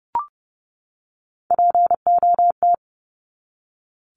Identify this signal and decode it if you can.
Morse code, heard as a keyed tone, about 30 wpm, 700 Hz. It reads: POT
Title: Top 751-1000 Words 30wpm